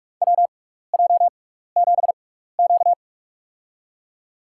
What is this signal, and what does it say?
Morse code, heard as a keyed tone, about 42 wpm, spaced as for 15 wpm, 700 Hz. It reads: WJ7Q
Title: USA Call Signs - 4 Characters - 6x Character Spacing - 45wpm